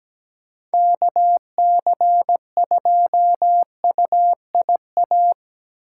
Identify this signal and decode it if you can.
Morse code, heard as a keyed tone, about 20 wpm, 700 Hz. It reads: KC2UIA
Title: Speed Racing - USA 2x3 Call Signs 17wpm